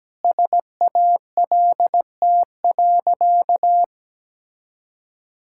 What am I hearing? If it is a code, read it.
Morse code, heard as a keyed tone, about 17 wpm, 700 Hz. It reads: SALT.
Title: Sentences from Top 1000 Words - Review Part 5 - 17wpm